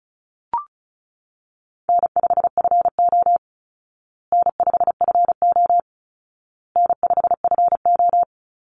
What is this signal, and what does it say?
Morse code, heard as a keyed tone, about 34 wpm, 700 Hz. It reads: N5FO N5FO N5FO
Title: Call Signs Repeated 2x - 35wpm